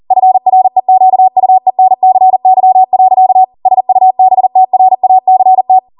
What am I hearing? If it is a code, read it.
Morse code, heard as a keyed tone, about 40 wpm, 750 Hz. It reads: FREQUENCY. SUBTRACT
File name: Morse / 40WPM